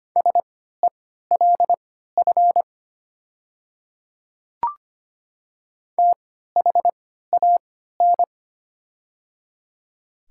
Morse code, heard as a keyed tone, about 25 wpm, spaced as for 14 wpm, 700 Hz. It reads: SELF THAN
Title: 3x Character Spacing - Top 200 Words - 25wpm